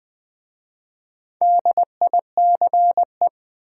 Morse code, heard as a keyed tone, about 20 wpm, 700 Hz. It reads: DICE
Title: Easy Words with Hints - 20wpm